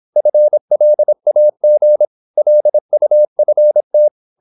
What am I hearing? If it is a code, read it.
Morse code, heard as a keyed tone, about 26 wpm, 600 Hz. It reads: FLAG LUFT
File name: Korte ord DK 26wpm